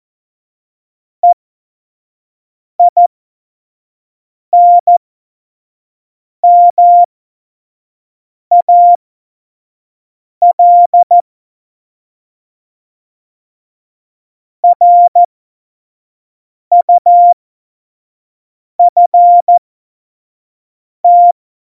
Morse code, heard as a keyed tone, about 14 wpm, spaced as for 5 wpm, 700 Hz. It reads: EINMAL RUFT
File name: Morsekurs / Witze / Witz 26. September 17, Tempo 25